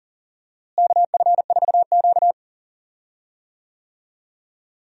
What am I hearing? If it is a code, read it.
Morse code, heard as a keyed tone, about 40 wpm, 700 Hz. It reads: KF4Q
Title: USA Call Signs - 4 Characters - Repeated 2x - 40wpm